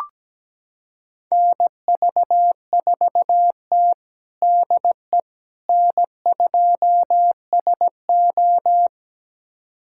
Morse code, heard as a keyed tone, about 17 wpm, 700 Hz. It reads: NV4T DE N2SO
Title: USA Callsign DE Callsign - 4 Characters - 17wpm